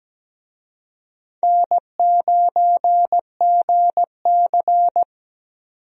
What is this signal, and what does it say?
Morse code, heard as a keyed tone, about 17 wpm, 700 Hz. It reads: N9GC
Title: USA Call Signs - 4 Characters - Repeated 3x - 17wpm